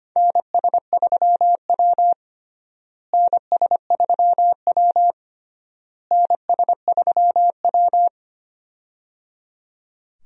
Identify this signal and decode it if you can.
Morse code, heard as a keyed tone, about 25 wpm, 700 Hz. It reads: NS3W NS3W NS3W
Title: USA Call Signs - 4 Characters - Repeated 3x - 25wpm